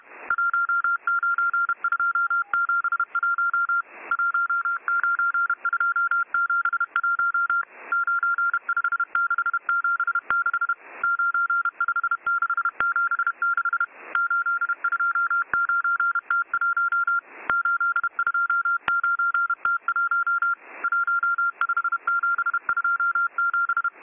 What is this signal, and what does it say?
Morse code, heard as a keyed tone, about 31 wpm, 1390 Hz. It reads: 19271 19271 95676 95676 829T1 829T1 15727